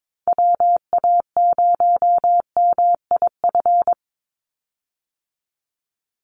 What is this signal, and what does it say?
Morse code, heard as a keyed tone, about 22 wpm, 700 Hz. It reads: WA0MIF
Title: USA Call Signs 22wpm